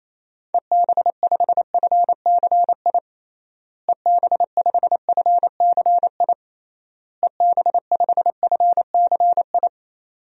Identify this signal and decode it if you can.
Morse code, heard as a keyed tone, about 28 wpm, 700 Hz. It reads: EB5FCI EB5FCI EB5FCI
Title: DXCC-100 Call Signs - 6 Characters - Repeated 3x - 28wpm